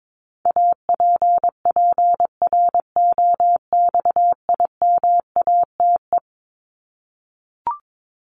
Morse code, heard as a keyed tone, about 22 wpm, 700 Hz. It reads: APPROXIMATE
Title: LY Suffix - Top 100 Words with Base Words 22wpm